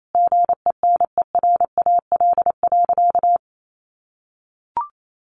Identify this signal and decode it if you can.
Morse code, heard as a keyed tone, about 28 wpm, 700 Hz. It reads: GENERAL.
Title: Sentences from Top 1500 Words 28wpm